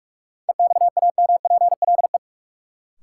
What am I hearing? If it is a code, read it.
Morse code, heard as a keyed tone, about 43 wpm, 700 Hz. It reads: EXAMPLE
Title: Sets of 2 Words - New Word Drill - 45wpm